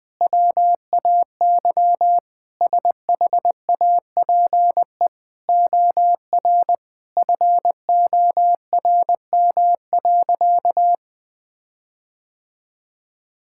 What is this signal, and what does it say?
Morse code, heard as a keyed tone, about 20 wpm, 700 Hz. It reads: WAY SHAPE OR FORM.
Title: Sentences with 6-Word Phrases - Spoken Beginning - 20wpm